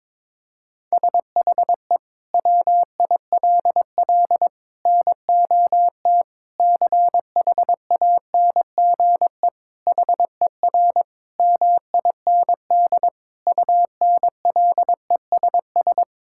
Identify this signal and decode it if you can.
Morse code, heard as a keyed tone, about 22 wpm, 700 Hz. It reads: SHE WILL NOT CHANGE HER MIND UNLESS
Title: Sentences from Top 1000 Words - Review Part 6 - 22wpm